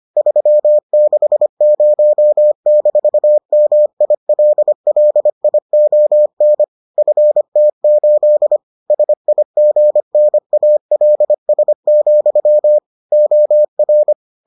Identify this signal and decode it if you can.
Morse code, heard as a keyed tone, about 25 wpm, 600 Hz. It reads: 360-MILLION FT8 SIGNALS, OR